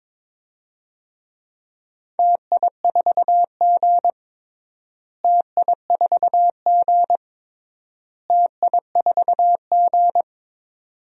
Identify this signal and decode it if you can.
Morse code, heard as a keyed tone, about 21 wpm, 700 Hz. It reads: TI4G TI4G TI4G
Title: DXCC-100 Call Signs - 4 Characters - Repeated 3x - 22wpm